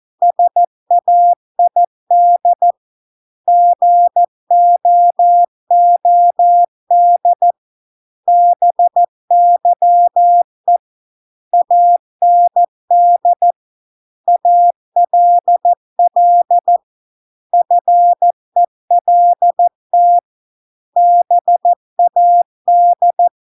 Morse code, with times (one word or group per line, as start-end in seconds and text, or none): 0.2-2.7 SAID
3.5-7.5 GOOD
8.3-10.8 BYE
11.5-13.5 AND
14.3-16.8 ALL
17.5-20.2 FELT
21.0-23.3 BAD